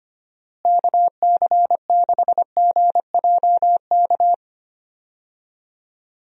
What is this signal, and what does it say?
Morse code, heard as a keyed tone, about 25 wpm, 700 Hz. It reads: KC6GJK
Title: USA Call Signs - 6 Characters - Repeated 2x - 25wpm